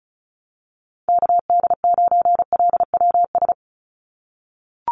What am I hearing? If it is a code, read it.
Morse code, heard as a keyed tone, about 36 wpm, 700 Hz. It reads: KD9LWS
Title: USA Call Signs 35wpm